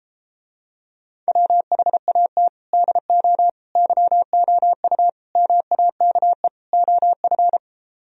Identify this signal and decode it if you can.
Morse code, heard as a keyed tone, about 33 wpm, 700 Hz. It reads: WHAT DO YOU MAKE OF